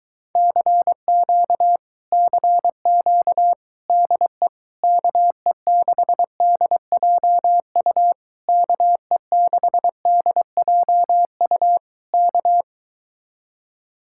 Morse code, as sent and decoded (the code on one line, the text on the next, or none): -.-. --.- / -.-. --.- / -.. / -.- . -.... -.. .--- ..- / -.- . -.... -.. .--- ..- / -.-
CQ CQ DE KE6DJU KE6DJU K